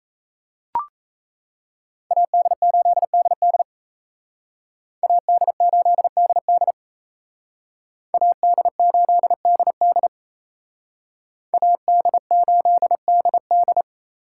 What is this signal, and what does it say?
Morse code, heard as a keyed tone, about 35 wpm, 700 Hz. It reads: AD8DD AD8DD AD8DD AD8DD
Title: Speed Racing - USA 2x2 Call Signs 28wpm